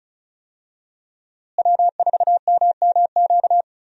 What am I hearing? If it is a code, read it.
Morse code, heard as a keyed tone, about 36 wpm, 700 Hz. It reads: W4MMQ
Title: USA Call Signs - 5 Characters - Repeated 3x - 35wpm